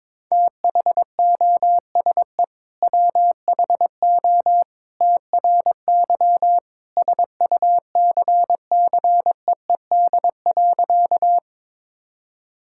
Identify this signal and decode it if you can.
Morse code, heard as a keyed tone, about 21 wpm, 700 Hz. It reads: THOSE WHO TRY SUCCEED.
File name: practice / 22wpm / Sentences with 2-Word Phrases - 22wpm